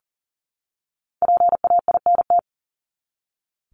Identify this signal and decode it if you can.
Morse code, heard as a keyed tone, about 40 wpm, 700 Hz. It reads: PAINT
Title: ING Suffix - Top 100 Words with Base Words 40wpm